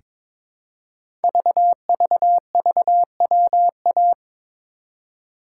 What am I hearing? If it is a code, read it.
Morse code, heard as a keyed tone, about 23 wpm, 700 Hz. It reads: VVVWA
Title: Code Groups - 5 Letters - 22wpm